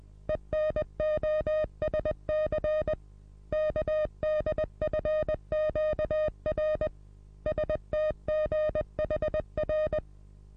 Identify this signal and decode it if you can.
Morse code, heard as a keyed tone, about 20 wpm, 620 Hz. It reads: ENOSC KDFQR STGHR
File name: repos / Morsekurs / DJ4UF Morsen27